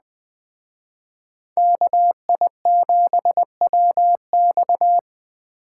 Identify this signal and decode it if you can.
Morse code, heard as a keyed tone, about 21 wpm, 700 Hz. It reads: KI7WX